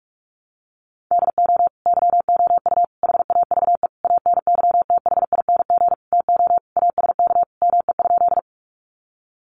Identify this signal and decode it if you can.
Morse code, heard as a keyed tone, about 44 wpm, 700 Hz. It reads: DO YOU HAVE ANYTHING TO ASK ME?
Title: Sentences from Top 1000 Words 45wpm